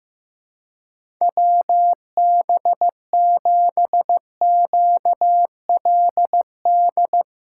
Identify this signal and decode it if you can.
Morse code, heard as a keyed tone, about 15 wpm, 700 Hz. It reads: WB7QLD
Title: USA Call Signs - 6 Characters - 4x Character Spacing - 15wpm